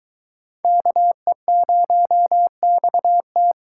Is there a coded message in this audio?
Yes: Morse code from 0.6 to 3.5 s, heard as a keyed tone, about 24 wpm, 700 Hz. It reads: KE0XT